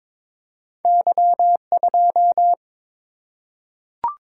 Morse code, heard as a keyed tone, about 22 wpm, 700 Hz. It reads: Y2
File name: practice / 22wpm / Two Letter-Number Combinations - 22wpm